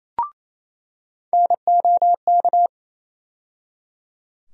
Morse code, heard as a keyed tone, about 28 wpm, 700 Hz. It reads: NOK
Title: USA Call Sign Postfixes - 3 Letters 28wpm